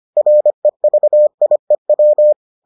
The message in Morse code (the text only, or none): REVIEW